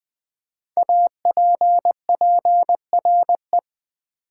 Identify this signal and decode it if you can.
Morse code, heard as a keyed tone, about 20 wpm, 700 Hz. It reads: APPRE
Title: Build A Word - 5 Letters At A Time - 12-Letter Words - 20wpm